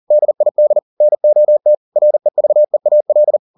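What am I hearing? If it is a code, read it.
Morse code, heard as a keyed tone, about 40 wpm, 600 Hz. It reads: DID NOT REVEAL
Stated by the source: Morse code at 40 wpm, practice text, CW med Gnister.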